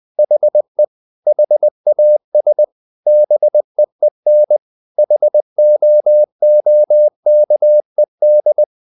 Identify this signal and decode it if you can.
Morse code, heard as a keyed tone, about 20 wpm, 600 Hz. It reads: HE HAS BEEN HOOKED
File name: the old man and the sea / the old man and the sea part 1 20wpm